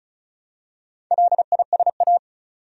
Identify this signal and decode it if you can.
Morse code, heard as a keyed tone, about 36 wpm, 700 Hz. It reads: LISA
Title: USA Common Names - 35wpm